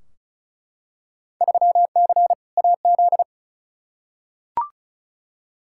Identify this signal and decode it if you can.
Morse code, heard as a keyed tone, about 34 wpm, 700 Hz. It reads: 3C AZ
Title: Field Day Exchanges Encore - 35wpm